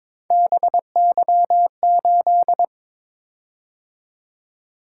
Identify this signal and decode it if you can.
Morse code, heard as a keyed tone, about 22 wpm, 700 Hz. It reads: BY8
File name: practice / 22wpm / Three Letter-Number Combinations - 22wpm